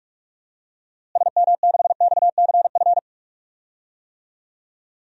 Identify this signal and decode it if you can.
Morse code, heard as a keyed tone, about 46 wpm, 700 Hz. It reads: SM6XCF
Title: DXCC-100 Call Signs - 6 Characters - Repeated 3x - 45wpm